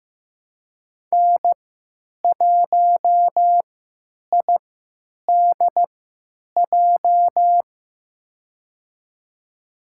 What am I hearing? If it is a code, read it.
Morse code, heard as a keyed tone, about 15 wpm, spaced as for 9 wpm, 700 Hz. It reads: N1IDJ
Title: USA Call Signs - 3x Character Spacing - 15wpm